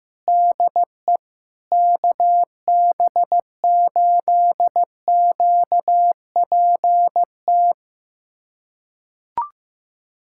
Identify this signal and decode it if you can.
Morse code, heard as a keyed tone, about 15 wpm, 700 Hz. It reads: DE KB8QPT
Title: USA Callsign DE Callsign - 6 Characters - 15wpm